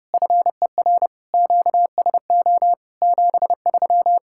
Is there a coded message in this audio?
Yes: Morse code from 0.1 to 4.2 s, heard as a keyed tone, about 30 wpm, 700 Hz. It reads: FER QSO 73